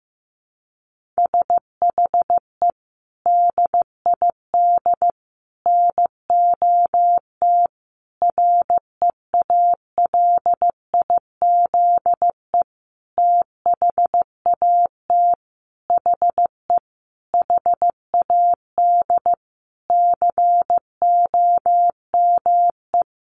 Morse code, heard as a keyed tone, about 15 wpm, 700 Hz. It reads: SHE DID NOT REALIZE THAT HE HAD COME
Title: Sentences from Top 1000 Words 15wpm